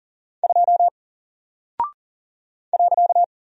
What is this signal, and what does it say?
Morse code, heard as a keyed tone, about 40 wpm, 700 Hz. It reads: 2 .
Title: Single Character (Letter:Number:Punctuation:Prosign) Alt 40wpm